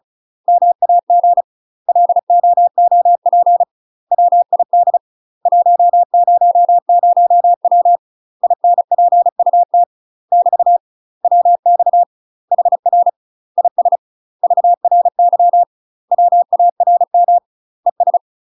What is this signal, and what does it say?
Morse code, heard as a keyed tone, about 36 wpm, 700 Hz. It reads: MAG LOOP WID 100W INPUT <BT> WX HR IS VRY WARM ES
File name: UK MASTER QSOs TRAINING / 35WPM / UK MASTER QSOs 0036